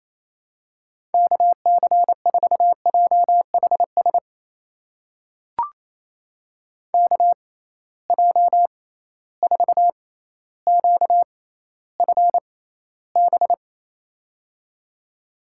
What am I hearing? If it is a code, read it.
Morse code, heard as a keyed tone, about 28 wpm, spaced as for 14 wpm, 700 Hz. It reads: KC4JHS KJ4QFB
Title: USA Call Signs - 6 Characters - 6x Character Spacing - 28wpm